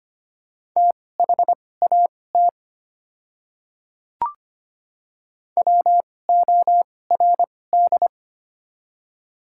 Morse code, heard as a keyed tone, about 25 wpm, spaced as for 18 wpm, 700 Hz. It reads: THAT WORD
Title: 2x Character Spacing - Top 300 Words - 25wpm